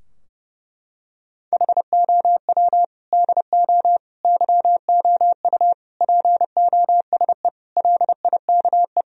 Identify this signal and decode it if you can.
Morse code, heard as a keyed tone, about 28 wpm, 700 Hz. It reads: HOW DO YOU POSE LIKE
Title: Sentences from Top 1000 Words - Review Part 2 - 30wpm